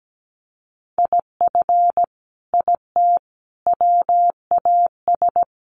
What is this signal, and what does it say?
Morse code, heard as a keyed tone, about 17 wpm, 700 Hz. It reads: IF IT WAS
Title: Sets of 5 Words Encore 17wpm